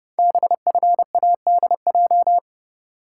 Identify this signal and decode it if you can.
Morse code, heard as a keyed tone, about 30 wpm, 700 Hz. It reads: BFADJ
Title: Code Groups - 5 Letters - 30wpm